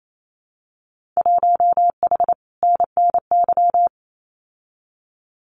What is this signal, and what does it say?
Morse code, heard as a keyed tone, about 28 wpm, 700 Hz. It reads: 1H NNY
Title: Winter Field Day Exchanges 28wpm